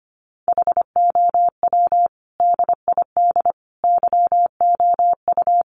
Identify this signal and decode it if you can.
Morse code, heard as a keyed tone, about 25 wpm, 700 Hz. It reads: HOW DID YOU
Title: Sentences from Top 1100 Words 25wpm